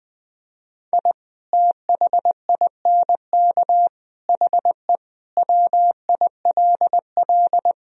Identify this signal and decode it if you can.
Morse code, heard as a keyed tone, about 20 wpm, 700 Hz. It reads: I THINK HE WILL
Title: Sentences from Top 1000 Words - Review Part 3 - 20wpm